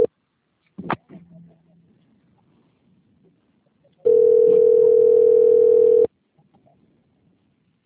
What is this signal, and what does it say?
Ringback tone (North America), 440 + 480 Hz.
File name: out- - - - - .93633